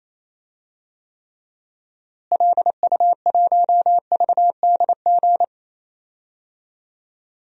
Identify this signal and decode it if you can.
Morse code, heard as a keyed tone, about 28 wpm, 700 Hz. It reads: LU1VDG
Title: DXCC-100 Call Signs - 6 Characters - Repeated 2x - 28wpm